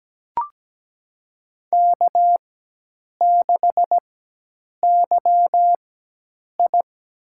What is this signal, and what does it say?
Morse code, heard as a keyed tone, about 17 wpm, spaced as for 8 wpm, 700 Hz. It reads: K6YI
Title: USA Call Signs - 4x Character Spacing - 17wpm